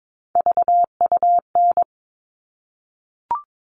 Morse code, heard as a keyed tone, about 22 wpm, 700 Hz. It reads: VUN
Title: All Cut Numbers - 3-Digit 22wpm